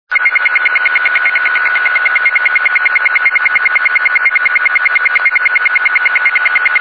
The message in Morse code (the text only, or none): none